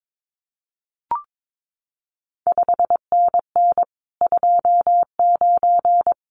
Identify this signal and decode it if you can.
Morse code, heard as a keyed tone, about 22 wpm, 700 Hz. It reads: 5NN 29
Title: CQ WW DX Contest Exchange Encore - 22wpm